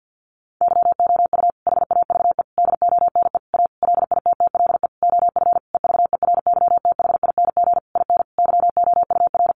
Morse code, heard as a keyed tone, about 50 wpm, 700 Hz. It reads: YOU HAVE DONE A LITTLE OF EVERYTHING IN YOUR
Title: Sentences from Top 700 Words 50wpm